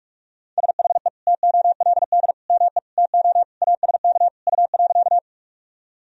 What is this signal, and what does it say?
Morse code, heard as a keyed tone, about 46 wpm, 700 Hz. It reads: SHE TOLD ME TO ASK U.
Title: Sentences from Top 400 Words and Abbreviations 45wpm